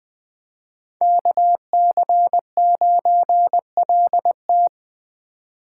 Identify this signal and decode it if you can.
Morse code, heard as a keyed tone, about 20 wpm, 700 Hz. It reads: KC9LT